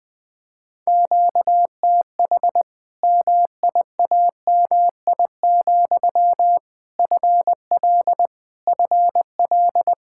Morse code, heard as a keyed tone, about 21 wpm, 700 Hz. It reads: QTH MIAMI, FL FL
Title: Practice QSO - Exchange 1 - Beginning - 20wpm